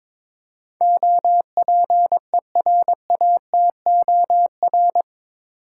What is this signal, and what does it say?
Morse code, heard as a keyed tone, about 21 wpm, 700 Hz. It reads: OPERATOR
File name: practice / 22wpm / Eight Letter Words - 22wpm